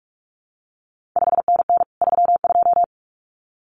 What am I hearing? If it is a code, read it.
Morse code, heard as a keyed tone, about 46 wpm, 700 Hz. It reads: 5NN 32
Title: CQ WW DX Contest Exchange - 45wpm